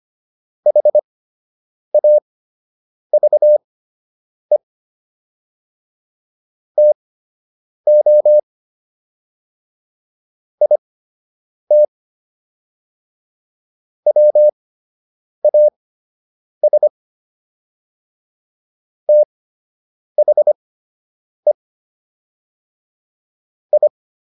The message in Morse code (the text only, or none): HAVE TO IT WAS THE I